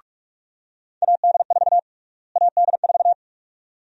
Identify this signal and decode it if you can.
Morse code, heard as a keyed tone, about 47 wpm, 700 Hz. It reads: AD4 AD4
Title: USA Call Sign Prefixes - Repeated 2x - 45wpm